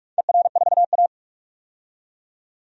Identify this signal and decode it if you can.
Morse code, heard as a keyed tone, about 42 wpm, 700 Hz. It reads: ER4A
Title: DXCC-100 Call Signs - 4 Characters - 45wpm